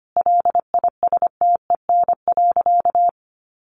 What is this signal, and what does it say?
Morse code, heard as a keyed tone, about 25 wpm, 700 Hz. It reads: LISTEN.
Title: Sentences from Top 400 Words 25wpm